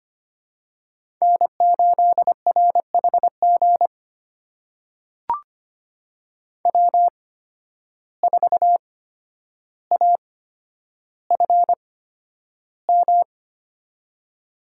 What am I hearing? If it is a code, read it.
Morse code, heard as a keyed tone, about 25 wpm, spaced as for 11 wpm, 700 Hz. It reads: N8RHG W4AFM
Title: USA Call Signs - 5 Characters - 8x Character Spacing - 25wpm